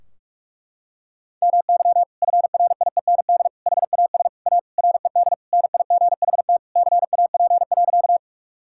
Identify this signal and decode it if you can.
Morse code, heard as a keyed tone, about 46 wpm, 700 Hz. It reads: MY FRIEND HAS A RED NIGHT CAP.
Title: Sentences from Top 1600 Words 45wpm